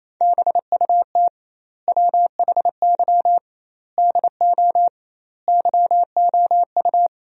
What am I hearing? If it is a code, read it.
Morse code, heard as a keyed tone, about 28 wpm, 700 Hz. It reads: BUT WHY DO YOU
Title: 2x Word-Spacing - Sentences from Top 200 Words - 28wpm